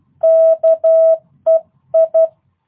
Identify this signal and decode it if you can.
Morse code, heard as a keyed tone, about 12 wpm, 650 Hz. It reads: KEI
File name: jh8cbh / 2022cw